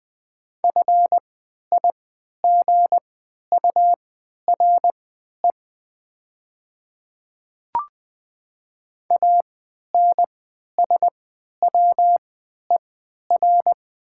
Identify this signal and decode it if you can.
Morse code, heard as a keyed tone, about 20 wpm, spaced as for 11 wpm, 700 Hz. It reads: FIGURE ANSWER
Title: 3x Character Spacing - Top 500 Words - 20wpm